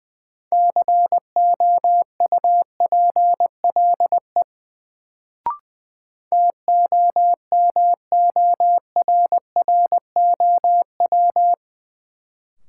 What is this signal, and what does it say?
Morse code, heard as a keyed tone, about 21 wpm, 700 Hz. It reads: COUPLE TOMORROW
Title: Sentences from Top 1100 Words - New Word Drill Alt - 20wpm